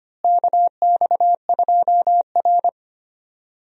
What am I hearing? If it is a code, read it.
Morse code, heard as a keyed tone, about 25 wpm, 700 Hz. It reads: KX2R
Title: USA Call Signs - 4 Characters - Repeated 3x - 25wpm